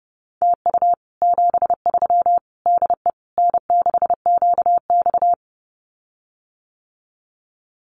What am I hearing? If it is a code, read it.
Morse code, heard as a keyed tone, about 30 wpm, 700 Hz. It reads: TU 73 DE N6QX
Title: Common QSO Phrases Encore 30wpm